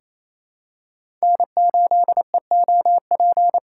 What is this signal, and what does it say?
Morse code, heard as a keyed tone, about 28 wpm, 700 Hz. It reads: N8EOP
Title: USA Call Signs - 2x Character Spacing - 28wpm